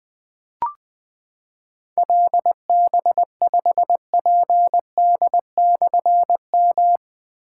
Morse code, heard as a keyed tone, about 20 wpm, 700 Hz. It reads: LB5PD/M
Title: Difficult Call Signs 20wpm